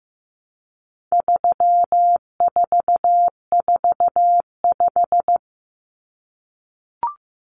Morse code, heard as a keyed tone, about 15 wpm, 700 Hz. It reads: 3445
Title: Numbers - 4-Digit 15wpm